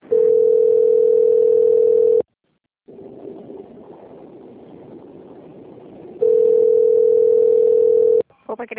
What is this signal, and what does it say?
Ringback tone (North America), 440 + 480 Hz, with speech.
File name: out- - - - - .61033